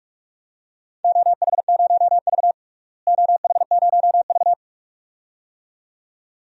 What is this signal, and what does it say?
Morse code, heard as a keyed tone, about 47 wpm, 700 Hz. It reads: OH0V OH0V
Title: DXCC-100 Call Signs - 4 Characters - Repeated 2x - 45wpm